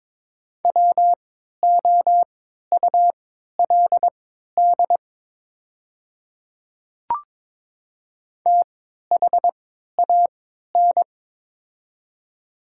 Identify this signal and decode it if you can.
Morse code, heard as a keyed tone, about 22 wpm, spaced as for 13 wpm, 700 Hz. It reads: WOULD THAN
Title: 3x Character Spacing - Top 100 Words - 22wpm